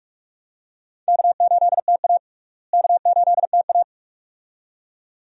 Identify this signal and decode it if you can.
Morse code, heard as a keyed tone, about 46 wpm, 700 Hz. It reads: K8TA K8TA
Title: USA Call Signs - 4 Characters - Repeated 2x - 45wpm